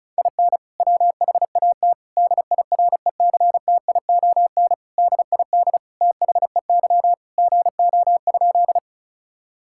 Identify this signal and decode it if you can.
Morse code, heard as a keyed tone, about 36 wpm, 700 Hz. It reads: IN WHAT DIRECTION DID THEY GO?
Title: Sentences from Top 1000 Words - Review Part 5 - 35wpm